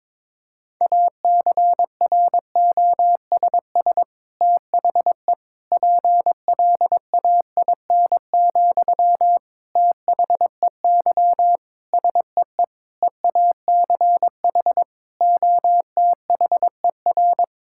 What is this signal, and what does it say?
Morse code, heard as a keyed tone, about 22 wpm, 700 Hz. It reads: ACROSS THE PLAIN, THEY SEE EACH OTHER
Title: Sentences from Top 1000 Words - Review Part 5 - 22wpm